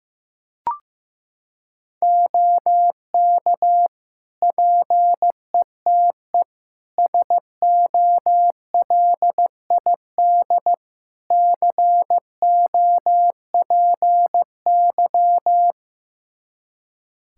Morse code, heard as a keyed tone, about 15 wpm, 700 Hz. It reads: OK PETE SOLID COPY
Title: Practice QSO - Exchange 3 - More Info - 15wpm